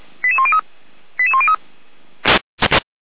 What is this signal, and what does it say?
Five-tone selective call (ZVEI), heard as quick runs of five tones, tones of about 70 ms. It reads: 88103 88103
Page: audio.f85360.feuerwehr-orlamuende.org